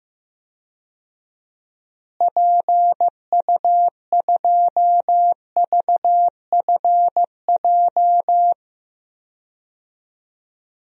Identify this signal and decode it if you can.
Morse code, heard as a keyed tone, about 15 wpm, 700 Hz. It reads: PU2VFJ
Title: DXCC-100 Call Signs - 6 Characters - Repeated 2x - 15wpm